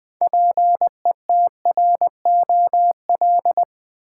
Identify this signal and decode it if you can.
Morse code, heard as a keyed tone, about 20 wpm, 700 Hz. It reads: PETROL
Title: Build A Word - 3 Letters At A Time - 6-Letter Words - 20wpm